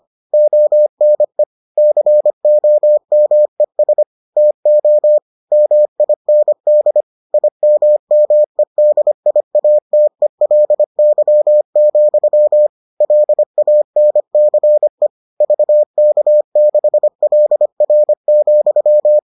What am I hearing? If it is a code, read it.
Morse code, heard as a keyed tone, about 25 wpm, 600 Hz. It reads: ONE COMES TO MIND IMMEDIATELY, LANCE VK6LR,